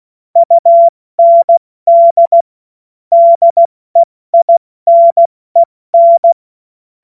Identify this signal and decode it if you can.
Morse code, heard as a keyed tone, about 16 wpm, 680 Hz. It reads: UND DEINEN
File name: Morsekurs / Witze / Witz 28.November 2023 70 F80